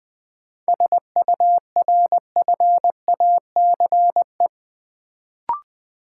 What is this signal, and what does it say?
Morse code, heard as a keyed tone, about 20 wpm, 700 Hz. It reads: SURFACE
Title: Sentences from Top 600 Words - New Word Drill - 20wpm